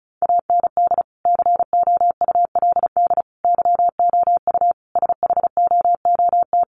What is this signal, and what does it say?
Morse code, heard as a keyed tone, about 34 wpm, 700 Hz. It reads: AND COULD YOU SHOOT
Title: Sentences from Top 1200 Words 35wpm